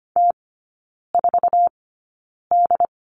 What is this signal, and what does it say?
Morse code, heard as a keyed tone, about 25 wpm, spaced as for 9 wpm, 700 Hz. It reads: T4D
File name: Lesson 17e - Letter u - Three Characters 25wpm